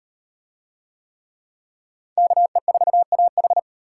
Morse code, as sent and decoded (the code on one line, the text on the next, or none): -.- . ....- .- ....
KE4AH